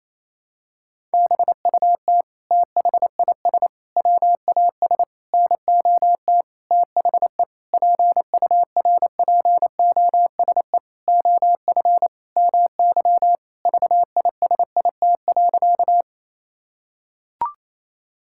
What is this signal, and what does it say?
Morse code, heard as a keyed tone, about 28 wpm, 700 Hz. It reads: BUT THIS WAS NOT THE PURPOSE OF MY VISIT.
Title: Sentences from Top 1000 Words - Review Part 6 - 28wpm